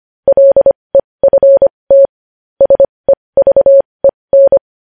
Morse code, heard as a keyed tone, about 25 wpm, spaced as for 20 wpm, 550 Hz. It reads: LEFT SEVEN
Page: news-headlines-in-morse-code-at-20-wpm